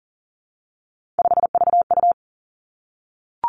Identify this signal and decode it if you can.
Morse code, heard as a keyed tone, about 40 wpm, 700 Hz. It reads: 5VU
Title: Three Challenging Letter-Number Combinations 40wpm